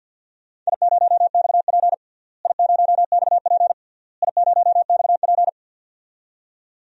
Morse code, heard as a keyed tone, about 51 wpm, 700 Hz. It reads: I0XP I0XP I0XP
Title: DXCC-100 Call Signs - 4 Characters - Repeated 3x - 50wpm